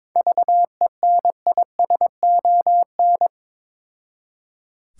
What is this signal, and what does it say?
Morse code, heard as a keyed tone, about 22 wpm, 700 Hz. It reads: VENISON
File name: Difficult Words with Hints - 22wpm